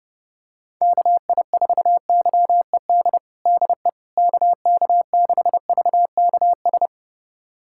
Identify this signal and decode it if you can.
Morse code, heard as a keyed tone, about 32 wpm, 700 Hz. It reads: KI4YED DE KK6VKS
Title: USA Callsign DE Callsign - 6 Characters - 30wpm